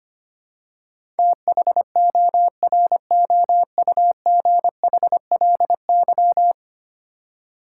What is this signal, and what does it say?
Morse code, heard as a keyed tone, about 25 wpm, 700 Hz. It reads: THOROUGHLY
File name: Ten Letter Words - 25wpm